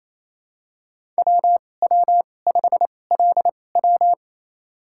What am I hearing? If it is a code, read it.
Morse code, heard as a keyed tone, about 28 wpm, spaced as for 20 wpm, 700 Hz. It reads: WW5LW
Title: USA Call Signs - 2x Character Spacing - 28wpm